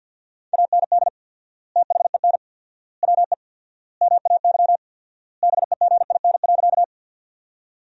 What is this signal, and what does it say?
Morse code, heard as a keyed tone, about 47 wpm, 700 Hz. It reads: AND THEN WE MAY BEGIN.
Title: 4x Word-Spacing - Sentences from Top 300 Words - 50wpm